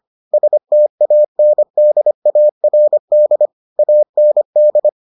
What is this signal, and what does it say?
Morse code, heard as a keyed tone, about 25 wpm, 600 Hz. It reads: STANDARD AND